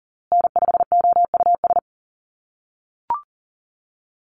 Morse code, heard as a keyed tone, about 40 wpm, 700 Hz. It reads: N5OUS
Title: USA Call Signs Repeated 2x 40wpm